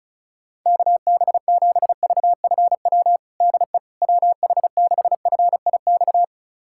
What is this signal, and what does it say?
Morse code, heard as a keyed tone, about 36 wpm, 700 Hz. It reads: KB7VFW DE WH6FIX
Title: USA Callsign DE Callsign - 6 Characters - 35wpm